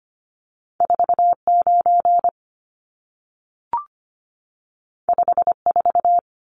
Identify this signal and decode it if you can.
Morse code, heard as a keyed tone, about 25 wpm, 700 Hz. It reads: 49 54
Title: Numbers - 2-Digit 25wpm